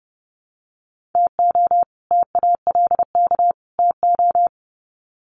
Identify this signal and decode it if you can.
Morse code, heard as a keyed tone, about 30 wpm, 700 Hz. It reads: TO TALK TO
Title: Sets of 3 Words 30wpm